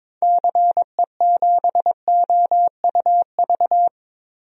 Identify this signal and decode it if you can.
Morse code, heard as a keyed tone, about 22 wpm, 700 Hz. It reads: CE7OUV
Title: DXCC-100 Call Signs - 6 Characters - Repeated 3x - 22wpm